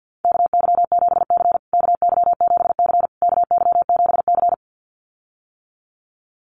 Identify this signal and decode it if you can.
Morse code, heard as a keyed tone, about 53 wpm, 700 Hz. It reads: KY7C KY7C KY7C
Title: USA Call Signs Repeated 3x 50wpm